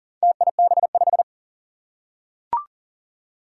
Morse code, heard as a keyed tone, about 37 wpm, 700 Hz. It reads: TIB5
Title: Four Challenging Letter-Number Combinations 40wpm